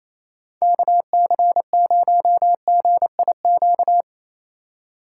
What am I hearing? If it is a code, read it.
Morse code, heard as a keyed tone, about 28 wpm, 700 Hz. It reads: KC0GIQ
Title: USA Call Signs - 6 Characters - 4x Character Spacing - 28wpm